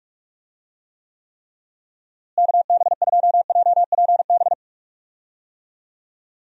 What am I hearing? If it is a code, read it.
Morse code, heard as a keyed tone, about 46 wpm, 700 Hz. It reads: KB2JPB
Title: USA Call Signs - 6 Characters - Repeated 3x - 45wpm